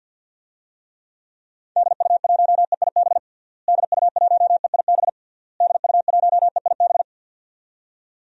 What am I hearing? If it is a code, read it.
Morse code, heard as a keyed tone, about 47 wpm, 700 Hz. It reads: DU1EIB DU1EIB DU1EIB
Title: DXCC-100 Call Signs - 6 Characters - Repeated 3x - 50wpm